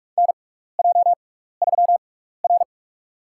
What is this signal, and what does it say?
Morse code, heard as a keyed tone, about 44 wpm, spaced as for 15 wpm, 700 Hz. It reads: NJ3R
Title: USA Call Signs - 6x Character Spacing - 45wpm